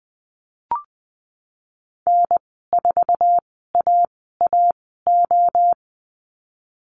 Morse code, heard as a keyed tone, about 20 wpm, spaced as for 15 wpm, 700 Hz. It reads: N4AAO
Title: 069 Lesson 12g - Letter d - Call Signs 20wpm